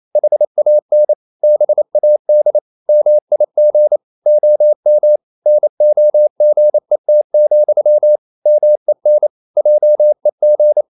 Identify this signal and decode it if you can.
Morse code, heard as a keyed tone, about 28 wpm, 600 Hz. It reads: HAN BAD MIG OM NOGET, MEN JEG
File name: Ferie QRQ 28wpm